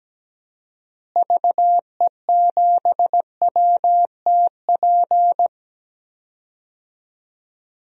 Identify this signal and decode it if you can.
Morse code, heard as a keyed tone, about 17 wpm, 700 Hz. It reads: VE7WTP
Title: DXCC-100 Call Signs - 6 Characters - Repeated 3x - 17wpm